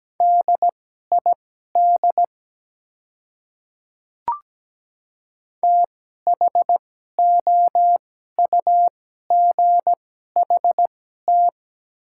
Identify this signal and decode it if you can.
Morse code, heard as a keyed tone, about 17 wpm, spaced as for 12 wpm, 700 Hz. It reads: DID THOUGHT
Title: 2x Character Spacing - Top 300 Words - 17wpm